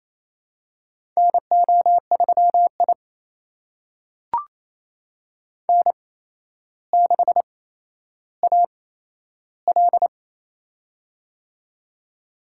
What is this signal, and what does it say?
Morse code, heard as a keyed tone, about 28 wpm, spaced as for 12 wpm, 700 Hz. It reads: NO3I N6AL
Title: USA Call Signs - 4 Characters - 8x Character Spacing - 28wpm